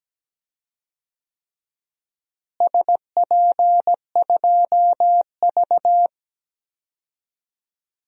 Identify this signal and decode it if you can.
Morse code, heard as a keyed tone, about 17 wpm, 700 Hz. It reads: SP2V
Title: DXCC-100 Call Signs - 4 Characters - Repeated 3x - 17wpm